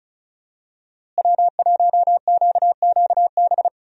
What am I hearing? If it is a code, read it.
Morse code, heard as a keyed tone, about 36 wpm, 700 Hz. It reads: W1QQB
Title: USA Call Signs - 5 Characters - Repeated 2x - 35wpm